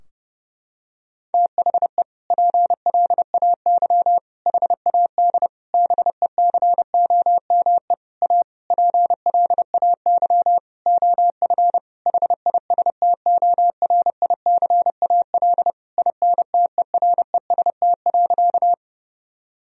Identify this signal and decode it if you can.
Morse code, heard as a keyed tone, about 30 wpm, 700 Hz. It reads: THE PLAY HAD BECOME A PLAY OF HISTORICAL INTEREST.
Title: Sentences from Top 1000 Words - Review Part 4 - 30wpm